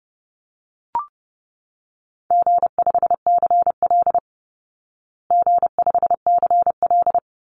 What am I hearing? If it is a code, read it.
Morse code, heard as a keyed tone, about 30 wpm, 700 Hz. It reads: G5CL G5CL
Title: Easy Call Signs Repeated 2x - 30wpm